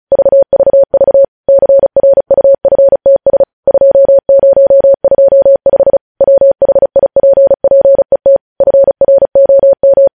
Morse code, heard as a keyed tone, about 36 wpm, 550 Hz. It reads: VVV CRUFTS 2025 WHIPPET FROM
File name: podcast / News headlines-1x-35x35